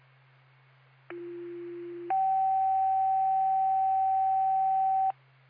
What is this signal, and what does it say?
Two-tone page, heard as a one-second tone, then a three-second tone, about 350 then 770 Hz. It is heard from 1.1 to 5.1 s.